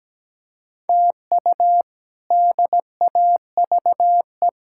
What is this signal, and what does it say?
Morse code, heard as a keyed tone, about 17 wpm, 700 Hz. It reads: TU DAVE
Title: K1USN SST QSO Practice - 17wpm